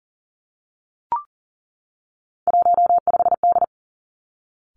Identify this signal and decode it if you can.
Morse code, heard as a keyed tone, about 38 wpm, 700 Hz. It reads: 15D
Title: Three Challenging Letter-Number Combinations 40wpm